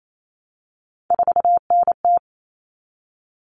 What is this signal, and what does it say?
Morse code, heard as a keyed tone, about 28 wpm, 700 Hz. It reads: 4NT
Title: Common Cut Numbers - 3-Digit 28wpm